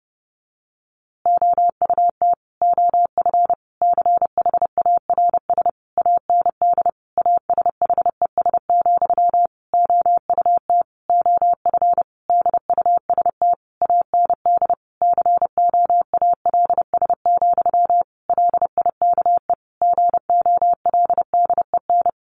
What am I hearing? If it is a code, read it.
Morse code, heard as a keyed tone, about 30 wpm, 700 Hz. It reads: OUT OF CHARS AND ASHES, OUT OF DUST AND COALS, LIKE GOLDEN
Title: A Sound Of Thunder 30wpm